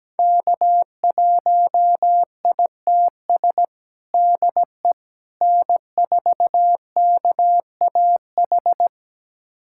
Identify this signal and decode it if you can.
Morse code, heard as a keyed tone, about 17 wpm, 700 Hz. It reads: K1ITS DE N4KAH
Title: USA Callsign DE Callsign - 5 Characters - 17wpm